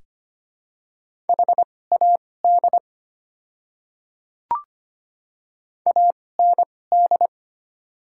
Morse code, heard as a keyed tone, about 25 wpm, spaced as for 18 wpm, 700 Hz. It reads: HAD AND
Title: 2x Character Spacing - Top 100 Words - 25wpm